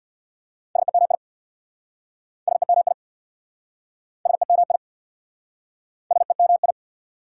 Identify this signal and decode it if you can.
Morse code, heard as a keyed tone, about 69 wpm, 690 Hz. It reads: SEMI SEMI SEMI SEMI